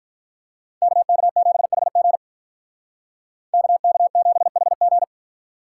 Morse code, heard as a keyed tone, about 46 wpm, 700 Hz. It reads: KK7HG KK7HG